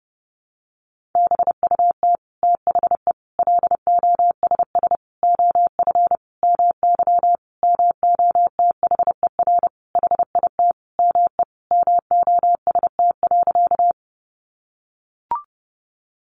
Morse code, heard as a keyed tone, about 30 wpm, 700 Hz. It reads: BUT THE LOSS OF MY MOTHER HIT ME MOST.
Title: Sentences from Top 1400 Words 30wpm